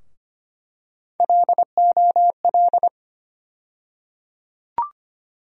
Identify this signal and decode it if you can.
Morse code, heard as a keyed tone, about 25 wpm, 700 Hz. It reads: LOL
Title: USA Call Sign Postfixes - 3 Letters 25wpm